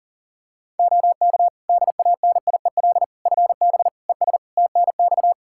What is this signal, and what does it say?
Morse code, heard as a keyed tone, about 40 wpm, 700 Hz. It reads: OK DANIEL FB ES TNX
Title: K1USN SST QSO Practice 40wpm